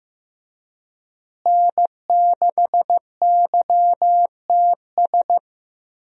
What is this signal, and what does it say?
Morse code, heard as a keyed tone, about 15 wpm, 700 Hz. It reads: N6YTS
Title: USA Call Signs - 5 Characters - Repeated 3x - 15wpm